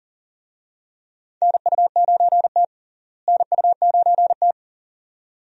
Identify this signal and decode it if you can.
Morse code, heard as a keyed tone, about 37 wpm, 700 Hz. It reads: NU9T NU9T
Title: USA Call Signs - 4 Characters - Repeated 2x - 40wpm